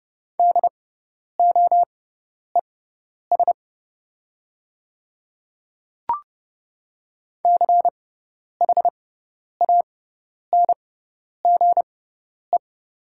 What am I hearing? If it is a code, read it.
Morse code, heard as a keyed tone, about 30 wpm, spaced as for 10 wpm, 700 Hz. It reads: DOES CHANGE
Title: 6x Character Spacing - Top 200 Words - 30wpm